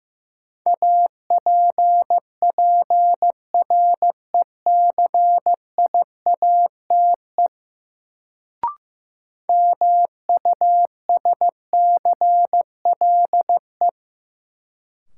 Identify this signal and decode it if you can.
Morse code, heard as a keyed tone, about 15 wpm, 700 Hz. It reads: APPRECIATE MUSCLE
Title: Sentences from Top 1300 Words - New Word Drill - 15wpm